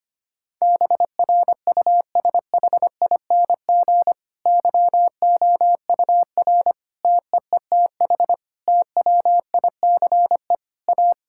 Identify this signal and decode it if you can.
Morse code, heard as a keyed tone, about 25 wpm, 700 Hz. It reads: BRUSHING YOUR TEETH TWICE A
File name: practice / 25wpm / Difficult Sentences with Hints - 25wpm